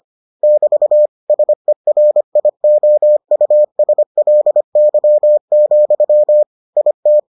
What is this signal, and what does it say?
Morse code, heard as a keyed tone, about 25 wpm, 600 Hz. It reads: <BT> SERIOUSLY, IT